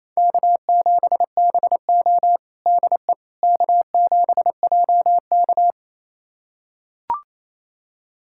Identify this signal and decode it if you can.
Morse code, heard as a keyed tone, about 28 wpm, 700 Hz. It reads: K7BO DE K7JK
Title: USA Callsign DE Callsign - 4 Characters - 28wpm